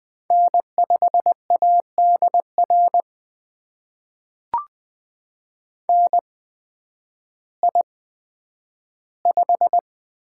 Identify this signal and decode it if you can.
Morse code, heard as a keyed tone, about 20 wpm, spaced as for 11 wpm, 700 Hz. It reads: N5ADR NI5
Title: USA Call Signs - 8x Character Spacing - 20wpm